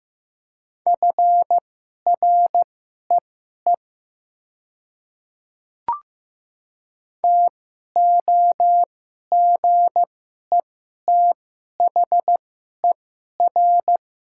Morse code, heard as a keyed tone, about 15 wpm, spaced as for 11 wpm, 690 Hz. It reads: FREE TOGETHER
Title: 2x Character Spacing - Top 500 Words - 15wpm